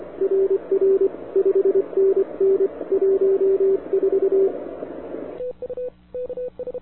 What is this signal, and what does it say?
Morse code, heard as a keyed tone, about 25 wpm, 390 Hz, with radio noise. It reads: RR 5NN 14
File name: wwcw08 / k5zd wwcw08 1700z